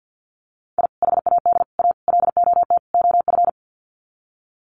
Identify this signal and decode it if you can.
Morse code, heard as a keyed tone, about 56 wpm, 700 Hz. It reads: I HAD A LOT OF
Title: Sets of 5 Words Encore 50wpm